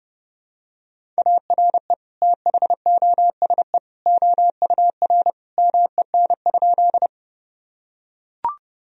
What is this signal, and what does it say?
Morse code, heard as a keyed tone, about 30 wpm, 700 Hz. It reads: ARE THOSE OUR MEN?
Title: Sentences from Top 1000 Words - Review Part 1 - 30wpm